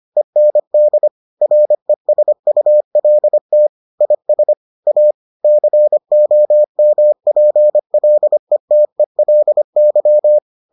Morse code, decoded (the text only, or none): END RESULT IS A COMPLETELY